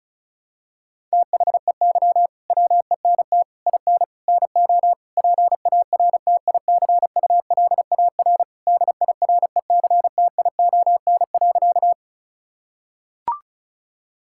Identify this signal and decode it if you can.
Morse code, heard as a keyed tone, about 34 wpm, 700 Hz. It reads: THEY WENT IN NO PARTICULAR DIRECTION.
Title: Sentences from Top 1000 Words - Review Part 5 - 35wpm